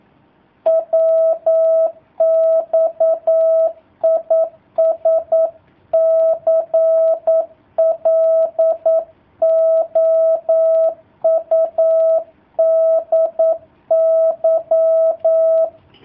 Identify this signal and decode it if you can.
Morse code, heard as a keyed tone, about 9 wpm, 650 Hz. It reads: WX IS CLOUDY
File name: jh8cbh / 2019cw / sp09- w